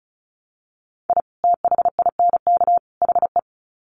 Morse code, heard as a keyed tone, about 34 wpm, 700 Hz. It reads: I THINK HE
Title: Sets of 3 Words 35wpm